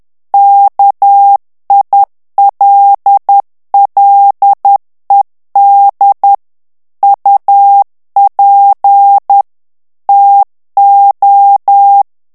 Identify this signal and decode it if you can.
Morse code, heard as a keyed tone, about 10 wpm, 800 Hz. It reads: KILLED UP TO
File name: morse mp3 / ANGLAIS texte clair 2 vitesse 10 mots minute